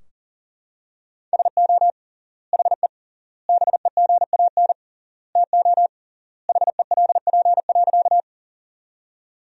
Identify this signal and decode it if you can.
Morse code, heard as a keyed tone, about 40 wpm, 700 Hz. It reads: SO HE BEGAN TO HELP.
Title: 3x Word-Spacing - Sentences from Top 300 Words - 40wpm